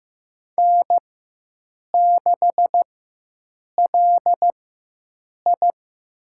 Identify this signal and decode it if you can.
Morse code, heard as a keyed tone, about 15 wpm, spaced as for 7 wpm, 700 Hz. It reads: N6LI
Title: USA Call Signs - 4 Characters - 4x Character Spacing - 15wpm